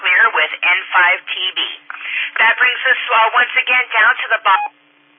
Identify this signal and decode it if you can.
Emergency Alert System (SAME) tones, heard as harsh buzzing bursts, with speech. It is heard from 0.8 to 4.4 s.
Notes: SMCARA Weekly Net on the 146.640 K3HKI Repeater in Lexington Park Maryland